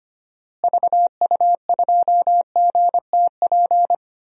Morse code, heard as a keyed tone, about 25 wpm, 700 Hz. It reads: VU2GTP
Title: DXCC-100 Call Signs - 6 Characters - Repeated 3x - 25wpm